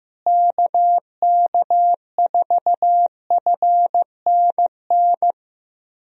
Morse code, heard as a keyed tone, about 15 wpm, 700 Hz. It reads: KK4FNN
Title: USA Call Signs - 6 Characters - Repeated 2x - 15wpm